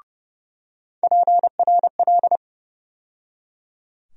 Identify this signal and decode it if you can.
Morse code, heard as a keyed tone, about 30 wpm, 700 Hz. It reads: PRL
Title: USA Call Sign Postfixes - 3 Letters 30wpm